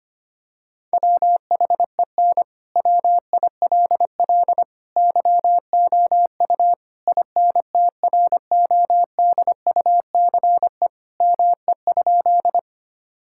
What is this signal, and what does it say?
Morse code, heard as a keyed tone, about 25 wpm, 700 Hz. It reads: WHEN WILL YOU INTRODUCE ME?
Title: Sentences from Top 1600 Words 25wpm